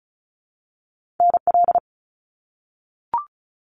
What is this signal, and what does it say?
Morse code, heard as a keyed tone, about 36 wpm, 700 Hz. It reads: NL
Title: Canadian Provinces and Territories 35wpm